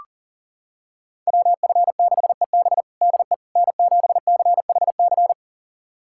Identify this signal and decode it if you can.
Morse code, heard as a keyed tone, about 40 wpm, 700 Hz. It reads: WF6EB DE N7CHC
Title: USA Callsign DE Callsign - 5 Characters - 40wpm